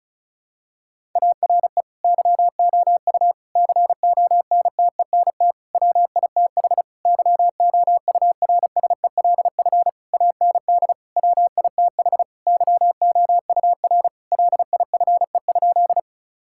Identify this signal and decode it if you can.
Morse code, heard as a keyed tone, about 36 wpm, 700 Hz. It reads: ARE YOU CONTENT WITH YOURSELF AND WITH YOUR LIFE?
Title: Sentences from Top 1000 Words - Review Part 6 - 35wpm